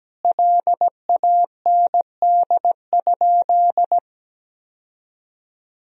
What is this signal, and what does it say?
Morse code, heard as a keyed tone, about 17 wpm, 700 Hz. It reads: LAND?
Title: Sentences from Top 1000 Words - Review Part 1 - 17wpm